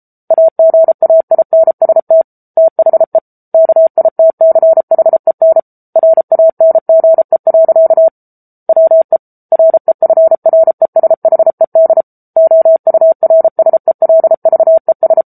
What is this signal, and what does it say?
Morse code, heard as a keyed tone, about 34 wpm, 650 Hz. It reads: AGAINST THE KITCHEN RANGE. WE REFRESHED OURSELVES
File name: Book Two / 07-Chapter 7 - 33 WPM